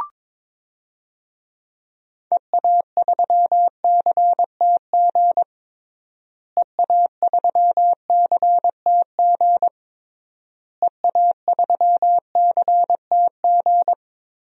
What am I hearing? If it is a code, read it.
Morse code, heard as a keyed tone, about 22 wpm, 700 Hz. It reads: EA3CTG EA3CTG EA3CTG
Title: DXCC-100 Call Signs - 6 Characters - Repeated 3x - 22wpm